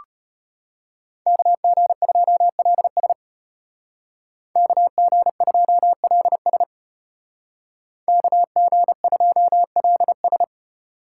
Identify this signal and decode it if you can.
Morse code, heard as a keyed tone, about 33 wpm, 700 Hz. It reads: KG2LS KG2LS KG2LS